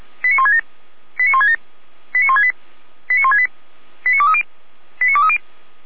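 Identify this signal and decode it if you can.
Five-tone selective call (ZVEI), heard as quick runs of five tones, tones of about 70 ms. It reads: 89167 89167 89168 89168 89230 89230
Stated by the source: ZVEI five-tone selective call received by SDR.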